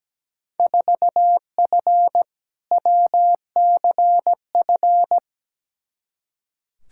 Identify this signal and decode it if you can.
Morse code, heard as a keyed tone, about 17 wpm, 700 Hz. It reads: 4F WCF
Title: Field Day Exchanges - 17wpm